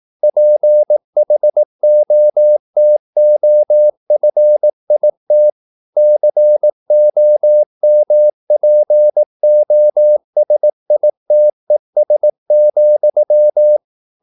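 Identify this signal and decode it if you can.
Morse code, heard as a keyed tone, about 18 wpm, 600 Hz. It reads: PHOTOFIT COMPOSITES,
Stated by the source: Morse practice at 18 wpm, English text, from CW med Gnister.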